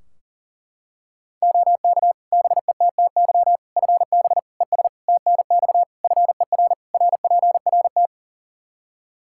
Morse code, heard as a keyed tone, about 40 wpm, 700 Hz. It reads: OK BETTY FB ES TNX FER RPRT
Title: Practice QSO - Exchange 2 - Info - 40wpm